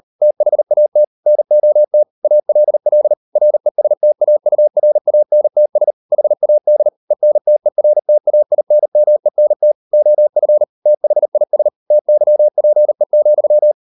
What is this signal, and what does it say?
Morse code, heard as a keyed tone, about 38 wpm, 600 Hz. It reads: THAT NOT ALL RESTAURANTS HAD ENTERTAINMENT OF THIS TYPE,